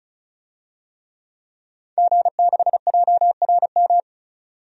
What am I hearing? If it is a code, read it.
Morse code, heard as a keyed tone, about 36 wpm, 700 Hz. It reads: G6JRM
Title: DXCC-100 Call Signs - 5 Characters - 35wpm